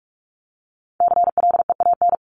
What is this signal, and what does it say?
Morse code, heard as a keyed tone, about 44 wpm, 700 Hz. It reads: CLEAN
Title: Top 1001-1250 Words 45wpm